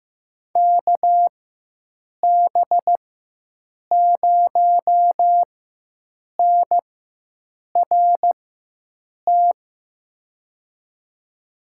Morse code, heard as a keyed tone, about 15 wpm, spaced as for 7 wpm, 700 Hz. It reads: KB0NRT
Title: USA Call Signs - 6 Characters - 4x Character Spacing - 15wpm